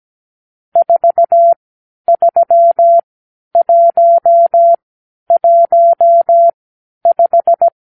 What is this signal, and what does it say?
Morse code, heard as a keyed tone, about 17 wpm, spaced as for 10 wpm, 680 Hz. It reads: 43115
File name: arrl / Review (1-5)